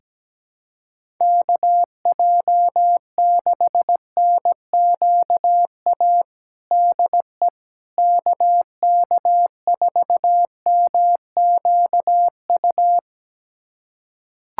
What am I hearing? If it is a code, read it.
Morse code, heard as a keyed tone, about 17 wpm, 700 Hz. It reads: KJ6NQA DE KK4MQU
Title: USA Callsign DE Callsign 17wpm